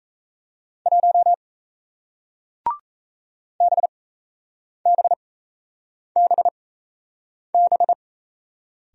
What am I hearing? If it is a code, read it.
Morse code, heard as a keyed tone, about 38 wpm, spaced as for 8 wpm, 700 Hz. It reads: 1 BBBB